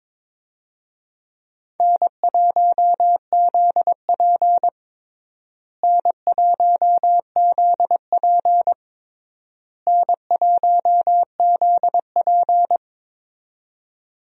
Morse code, heard as a keyed tone, about 22 wpm, 700 Hz. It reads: N1ZP N1ZP N1ZP
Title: USA Call Signs - 4 Characters - Repeated 3x - 22wpm